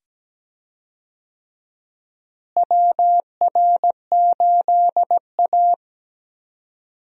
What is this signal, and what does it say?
Morse code, heard as a keyed tone, about 17 wpm, 700 Hz. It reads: WR8A
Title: USA Call Signs - 4 Characters - Repeated 2x - 17wpm